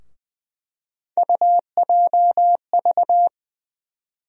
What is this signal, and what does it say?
Morse code, heard as a keyed tone, about 20 wpm, 700 Hz. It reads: UJV
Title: USA Call Sign Postfixes - 3 Letters 20wpm